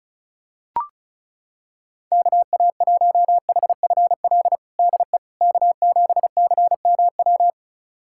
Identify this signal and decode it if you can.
Morse code, heard as a keyed tone, about 34 wpm, 700 Hz. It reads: KA1HFL DE K7CMW
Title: USA Callsign DE Callsign 35wpm